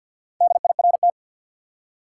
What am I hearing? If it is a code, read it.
Morse code, heard as a keyed tone, about 50 wpm, 700 Hz. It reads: DIRT
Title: Four Letter Words - 50wpm